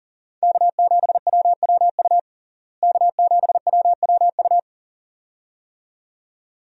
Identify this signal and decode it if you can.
Morse code, heard as a keyed tone, about 40 wpm, 700 Hz. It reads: K7WWU K7WWU
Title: USA Call Signs - 5 Characters - Repeated 2x - 40wpm